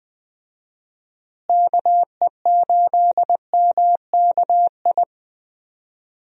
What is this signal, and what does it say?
Morse code, heard as a keyed tone, about 20 wpm, 700 Hz. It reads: KE8MKI
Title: USA Call Signs - 6 Characters - Repeated 3x - 20wpm